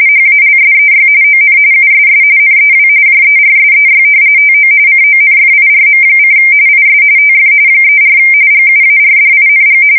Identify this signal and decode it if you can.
RTTY, about 45.45 baud. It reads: HALSEY ON USS NEW JERSEY TO CONFIRM: IS TF 34 GUARDING
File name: fsk170 / battleshipnj